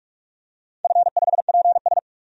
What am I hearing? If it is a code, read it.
Morse code, heard as a keyed tone, about 46 wpm, 700 Hz. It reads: F5PS
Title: DXCC-100 Call Signs - 4 Characters - 45wpm